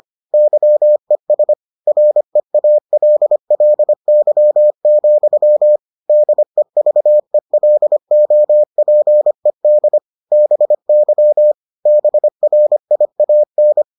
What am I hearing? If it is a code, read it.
Morse code, heard as a keyed tone, about 25 wpm, 600 Hz. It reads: YES REALLY, DEVELOPED BY BRIAN